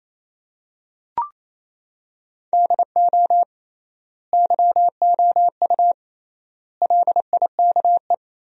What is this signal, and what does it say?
Morse code, heard as a keyed tone, about 28 wpm, 700 Hz. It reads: DO YOU LIKE
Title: 3x Word-Spacing - Sentences from Top 100 Words - 28wpm